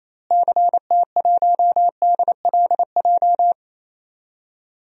Morse code, heard as a keyed tone, about 28 wpm, 700 Hz. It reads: CT1DLJ
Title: DXCC-100 Call Signs - 6 Characters - Repeated 3x - 28wpm